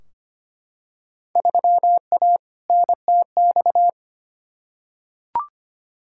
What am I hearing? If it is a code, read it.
Morse code, heard as a keyed tone, about 25 wpm, 700 Hz. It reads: 3A NTX
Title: Field Day Exchanges - 25wpm